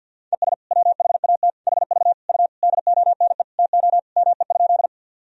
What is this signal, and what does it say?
Morse code, heard as a keyed tone, about 51 wpm, 700 Hz. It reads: ES WHAT HV U DONE TO ME?
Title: Sentences from Top 500 Words and Abbreviations 50wpm